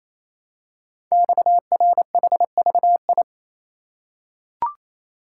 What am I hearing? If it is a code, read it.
Morse code, heard as a keyed tone, about 27 wpm, 700 Hz. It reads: XRHVI
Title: Code Groups - 5 Letters - 28wpm